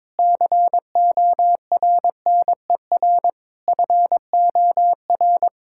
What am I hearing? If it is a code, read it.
Morse code, heard as a keyed tone, about 22 wpm, 700 Hz. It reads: CORNER FOR
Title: Sentences from Top 1600 Words 22wpm